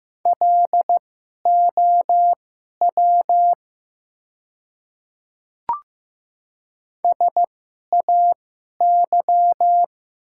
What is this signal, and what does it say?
Morse code, heard as a keyed tone, about 15 wpm, spaced as for 11 wpm, 700 Hz. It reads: LOW SAY
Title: 2x Character Spacing - Top 200 Words - 15wpm